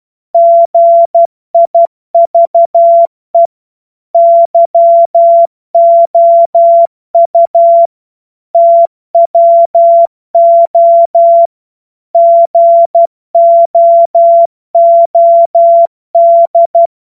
Morse code, heard as a keyed tone, about 12 wpm, 680 Hz. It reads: GIVE YOU TWO GOOD
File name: Morsekurs / Witze